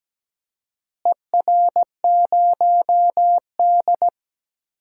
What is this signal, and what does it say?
Morse code, heard as a keyed tone, about 17 wpm, 700 Hz. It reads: ER0D
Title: DXCC-100 Call Signs - 4 Characters - Repeated 3x - 17wpm